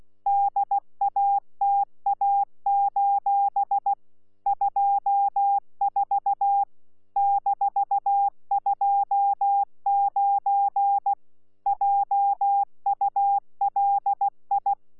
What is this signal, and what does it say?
Morse code, heard as a keyed tone, about 16 wpm, 800 Hz. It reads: DATA: 24 -29 JULI
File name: Z PI4VRZ / PI4VRZ 14-jun-2025